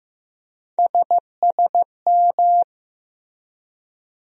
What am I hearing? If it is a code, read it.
Morse code, heard as a keyed tone, about 15 wpm, 700 Hz. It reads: SSM
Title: USA Call Sign Postfixes - 3 Letters 15wpm